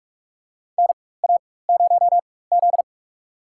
Morse code, heard as a keyed tone, about 42 wpm, spaced as for 20 wpm, 700 Hz. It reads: NA0Z
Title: USA Call Signs - 4 Characters - 4x Character Spacing - 45wpm